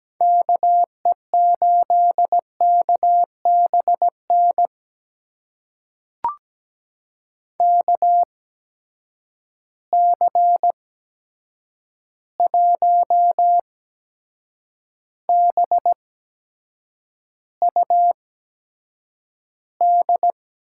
Morse code, heard as a keyed tone, about 17 wpm, spaced as for 7 wpm, 700 Hz. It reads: KE8KBN KC1BUD
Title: USA Call Signs - 6 Characters - 8x Character Spacing - 17wpm